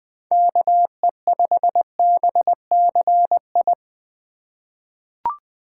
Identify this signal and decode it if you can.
Morse code, heard as a keyed tone, about 20 wpm, 700 Hz. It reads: KE5BCI
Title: USA Call Signs - 6 Characters - 4x Character Spacing - 20wpm